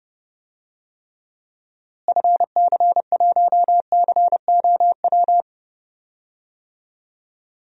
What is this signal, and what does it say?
Morse code, heard as a keyed tone, about 30 wpm, 700 Hz. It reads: FC1COW
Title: DXCC-100 Call Signs - 6 Characters - Repeated 3x - 30wpm